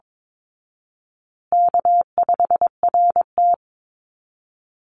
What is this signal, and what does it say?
Morse code, heard as a keyed tone, about 22 wpm, 700 Hz. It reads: K5RT
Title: Easy Call Signs Repeated 3x - 22wpm